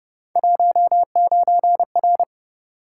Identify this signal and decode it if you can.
Morse code, heard as a keyed tone, about 32 wpm, 700 Hz. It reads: 19R